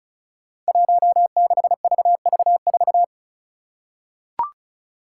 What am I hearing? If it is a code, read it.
Morse code, heard as a keyed tone, about 36 wpm, 700 Hz. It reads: 16VV4
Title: Five Challenging Letter-Number Combinations 35wpm